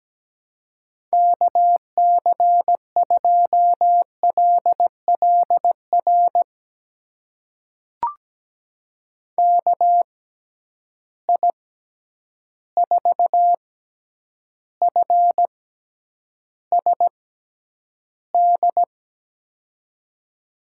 Morse code, heard as a keyed tone, about 17 wpm, spaced as for 9 wpm, 700 Hz. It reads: KC2LLR KI4FSD
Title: USA Call Signs - 6x Character Spacing - 17wpm